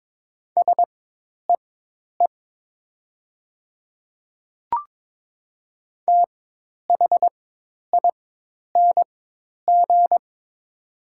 Morse code, heard as a keyed tone, about 22 wpm, spaced as for 10 wpm, 700 Hz. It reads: SEE THING
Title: 4x Character Spacing - Top 100 Words - 22wpm